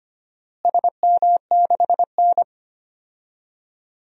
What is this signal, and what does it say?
Morse code, heard as a keyed tone, about 25 wpm, 700 Hz. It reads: SM6N
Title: DXCC-100 Call Signs - 4 Characters - Repeated 2x - 25wpm